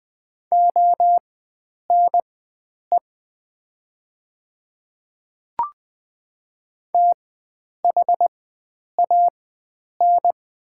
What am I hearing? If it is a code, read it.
Morse code, heard as a keyed tone, about 20 wpm, spaced as for 9 wpm, 700 Hz. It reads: ONE THAN
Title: 4x Character Spacing - Top 100 Words - 20wpm